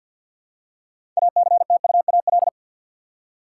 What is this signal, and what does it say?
Morse code, heard as a keyed tone, about 46 wpm, 700 Hz. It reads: ACTUAL
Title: Six Letter Words - 50wpm